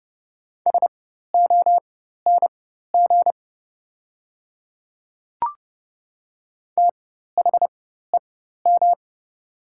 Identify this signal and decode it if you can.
Morse code, heard as a keyed tone, about 30 wpm, spaced as for 14 wpm, 700 Hz. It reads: SONG THEM
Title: 4x Character Spacing - Top 500 Words - 30wpm